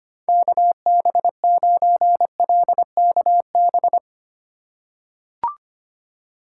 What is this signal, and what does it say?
Morse code, heard as a keyed tone, about 25 wpm, 700 Hz. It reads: KB9LKB
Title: USA Call Signs - 6 Characters - 4x Character Spacing - 25wpm